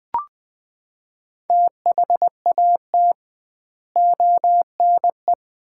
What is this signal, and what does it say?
Morse code, heard as a keyed tone, about 20 wpm, 700 Hz. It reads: THAT ONE
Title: 2x Word-Spacing - Sets of 4 Words - 20wpm